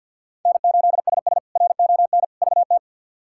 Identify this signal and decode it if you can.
Morse code, heard as a keyed tone, about 51 wpm, 700 Hz. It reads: N8SS RON VT
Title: K1USN SST QSO Practice - 50wpm